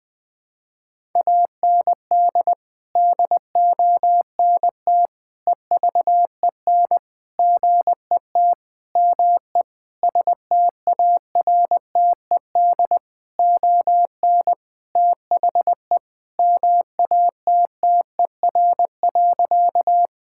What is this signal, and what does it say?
Morse code, heard as a keyed tone, about 20 wpm, 700 Hz. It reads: AND DONT EVEN GET ME STARTED ON THE MATTER.
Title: Sentences with 7-Word Phrases - Spoken Beginning - 20wpm